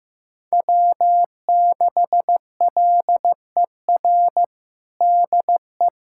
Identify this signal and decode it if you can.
Morse code, heard as a keyed tone, about 15 wpm, 700 Hz. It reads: W6LER DE
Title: USA Callsign DE Callsign 15wpm